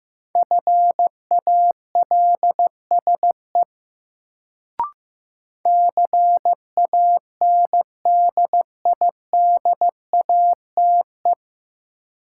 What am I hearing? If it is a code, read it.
Morse code, heard as a keyed tone, about 15 wpm, 700 Hz. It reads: FALSE CANDIDATE
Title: Sentences from Top 1600 Words - New Word Drill - 15wpm